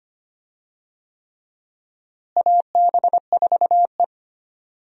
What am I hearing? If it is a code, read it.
Morse code, heard as a keyed tone, about 25 wpm, 700 Hz. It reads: AB4E
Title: USA Call Signs - 4 Characters - Repeated 3x - 25wpm